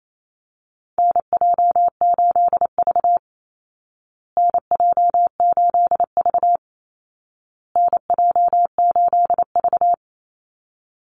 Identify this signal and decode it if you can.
Morse code, heard as a keyed tone, about 28 wpm, 700 Hz. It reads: NJ8V NJ8V NJ8V
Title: Call Signs Repeated 2x - 28wpm